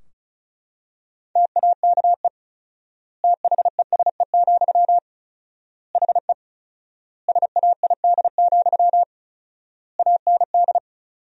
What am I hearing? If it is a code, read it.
Morse code, heard as a keyed tone, about 34 wpm, 700 Hz. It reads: TAKE THESE, HE SAID, AND
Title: 4x Word-Spacing - Sentences from Top 300 Words - 35wpm